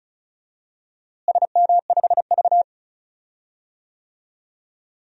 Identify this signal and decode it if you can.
Morse code, heard as a keyed tone, about 34 wpm, 700 Hz. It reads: SM5V
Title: DXCC-100 Call Signs - 4 Characters - Repeated 2x - 35wpm